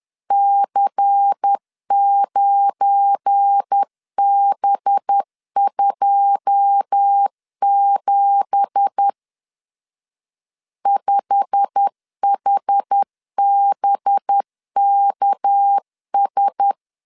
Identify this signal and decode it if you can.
Morse code, heard as a keyed tone, about 10 wpm, 800 Hz. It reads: C9B27 5HBKS
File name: LEZIONI / 9 2867 10wpm 2733